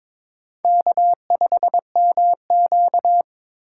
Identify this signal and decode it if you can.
Morse code, heard as a keyed tone, about 22 wpm, 700 Hz. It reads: K5MQ
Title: USA Call Signs - 3x Character Spacing - 22wpm